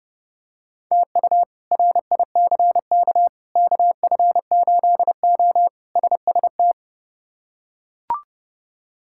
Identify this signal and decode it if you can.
Morse code, heard as a keyed tone, about 29 wpm, 700 Hz. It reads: TU RICK KF8O SST
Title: K1USN SST QSO Practice - 30wpm